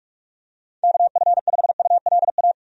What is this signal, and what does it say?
Morse code, heard as a keyed tone, about 46 wpm, 700 Hz. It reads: KF5ULA
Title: USA Call Signs - 6 Characters - 45wpm